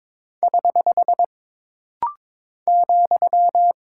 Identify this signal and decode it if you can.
Morse code, heard as a keyed tone, about 22 wpm, 700 Hz. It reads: <HH> ,
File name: practice / 22wpm / Single Punctuation-Prosign - Rapid-Fire - 22wpm